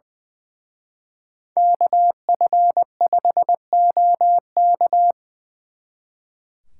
Morse code, heard as a keyed tone, about 24 wpm, 700 Hz. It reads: KF5OK
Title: Speed Racing - USA 2x2 Call Signs 20wpm